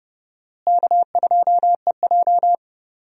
Morse code, heard as a keyed tone, about 30 wpm, 700 Hz. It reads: K2EJ
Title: USA Call Signs - 4 Characters - Repeated 3x - 30wpm